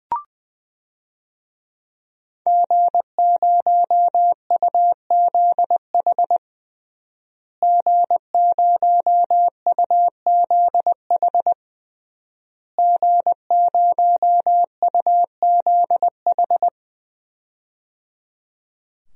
Morse code, heard as a keyed tone, about 21 wpm, 700 Hz. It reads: G0UZH G0UZH G0UZH
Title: DXCC-100 Call Signs - 5 Characters - Repeated 3x - 20wpm